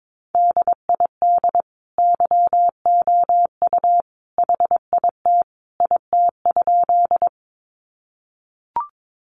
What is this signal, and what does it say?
Morse code, heard as a keyed tone, about 22 wpm, 700 Hz. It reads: DID YOU HIT IT?
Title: Sentences from Top 1200 Words 22wpm